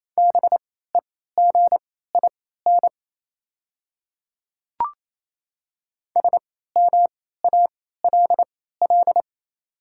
Morse code, heard as a keyed tone, about 28 wpm, spaced as for 16 wpm, 700 Hz. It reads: BEGIN SMALL
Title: 3x Character Spacing - Top 500 Words - 28wpm